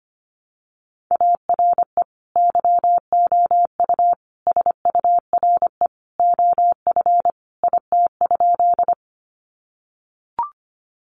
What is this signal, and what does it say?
Morse code, heard as a keyed tone, about 25 wpm, 700 Hz. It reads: ARE YOU SURE OF IT?
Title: Sentences from Top 300 Words 25wpm